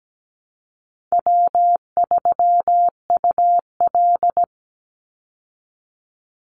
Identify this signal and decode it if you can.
Morse code, heard as a keyed tone, about 17 wpm, 700 Hz. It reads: W3UL
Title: Easy Call Signs Repeated 3x - 17wpm